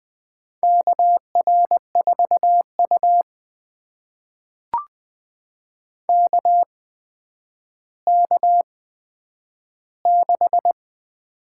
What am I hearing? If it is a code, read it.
Morse code, heard as a keyed tone, about 20 wpm, spaced as for 10 wpm, 700 Hz. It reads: KR4U KK6
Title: USA Call Signs - 8x Character Spacing - 20wpm